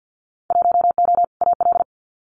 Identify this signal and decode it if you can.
Morse code, heard as a keyed tone, about 50 wpm, 700 Hz. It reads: 1O AL
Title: Winter Field Day Exchanges 50wpm